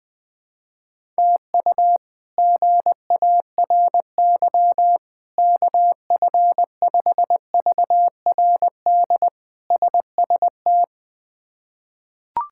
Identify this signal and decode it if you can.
Morse code, heard as a keyed tone, about 20 wpm, 700 Hz. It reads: TU GARY KF5VRD SST
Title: K1USN SST QSO Practice - 20wpm